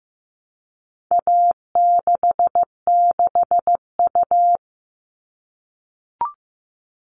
Morse code, heard as a keyed tone, about 15 wpm, 700 Hz. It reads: A66U
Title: All Cut Numbers - 4-Digit 15wpm